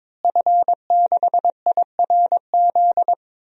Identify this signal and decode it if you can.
Morse code, heard as a keyed tone, about 21 wpm, 700 Hz. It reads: F6IRZ
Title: DXCC-100 Call Signs - 5 Characters - Repeated 2x - 22wpm